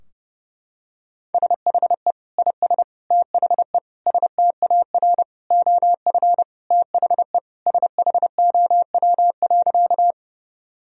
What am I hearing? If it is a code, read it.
Morse code, heard as a keyed tone, about 30 wpm, 700 Hz. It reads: SHE IS THE STAR OF THE SHOW.
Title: Sentences from Top 1000 Words - Review Part 2 - 30wpm